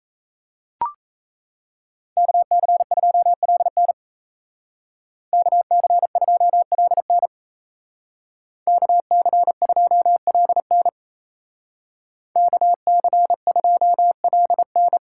Morse code, heard as a keyed tone, about 36 wpm, 700 Hz. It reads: KC2LN KC2LN KC2LN KC2LN